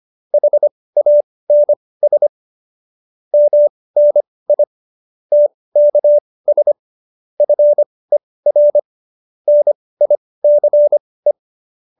Morse code, heard as a keyed tone, about 25 wpm, spaced as for 18 wpm, 570 Hz. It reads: HANS MNI TKS FER NICE